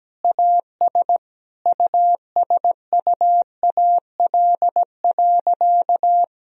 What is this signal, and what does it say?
Morse code, heard as a keyed tone, about 17 wpm, 700 Hz. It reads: AS USUAL.
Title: Sentences from Top 1000 Words - Review Part 2 - 17wpm